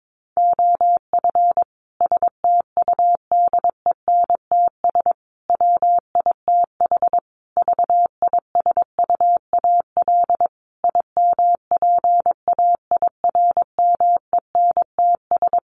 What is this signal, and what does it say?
Morse code, heard as a keyed tone, about 22 wpm, 700 Hz. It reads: OF STUDENTS WITH VISUAL IMPAIRMENTS
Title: Sets of 5 Words Encore 22wpm